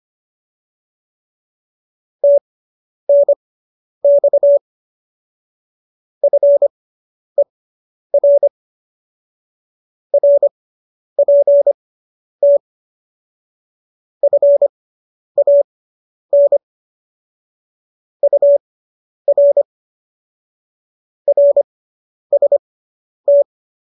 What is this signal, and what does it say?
Morse code, heard as a keyed tone, about 25 wpm, spaced as for 10 wpm, 570 Hz. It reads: TNX FER RPT FAN UR RST